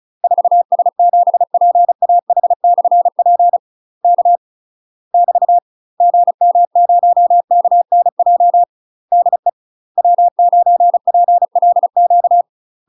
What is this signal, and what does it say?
Morse code, heard as a keyed tone, about 35 wpm, 700 Hz. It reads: 4S7PAH/P K <BT> GM0KNJ DE W9PLQ
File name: UK MASTER QSOs TRAINING / 35WPM / UK MASTER QSOs 0014